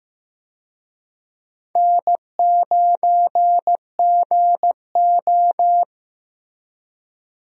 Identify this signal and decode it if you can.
Morse code, heard as a keyed tone, about 15 wpm, 700 Hz. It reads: N9GO
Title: USA Call Signs - 4 Characters - Repeated 3x - 15wpm